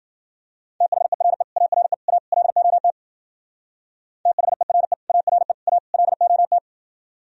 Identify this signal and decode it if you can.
Morse code, heard as a keyed tone, about 52 wpm, 700 Hz. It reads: THERE ARE A LOT THERE ARE A LOT